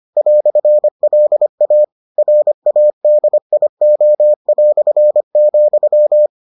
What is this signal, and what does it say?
Morse code, heard as a keyed tone, about 25 wpm, 600 Hz. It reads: "LA RADIO",